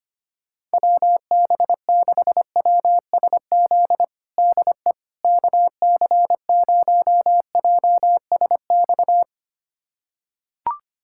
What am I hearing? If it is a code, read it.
Morse code, heard as a keyed tone, about 25 wpm, 700 Hz. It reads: WB6WSZ DE KC0JSX
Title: USA Callsign DE Callsign 25wpm